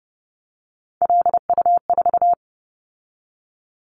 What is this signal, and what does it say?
Morse code, heard as a keyed tone, about 30 wpm, 700 Hz. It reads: LU4
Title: Callsign Prefixes for Top 50 DXCC - 30wpm